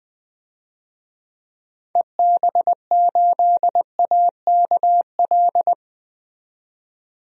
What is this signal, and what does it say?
Morse code, heard as a keyed tone, about 20 wpm, 700 Hz. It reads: EB8AKL
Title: DXCC-100 Call Signs - 6 Characters - 20wpm